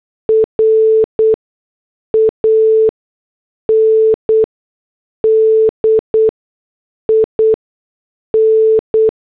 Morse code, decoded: RAND IN